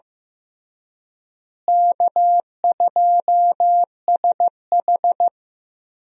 Morse code, heard as a keyed tone, about 15 wpm, 700 Hz. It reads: K2SH
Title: USA Call Signs - 4 Characters - Repeated 3x - 15wpm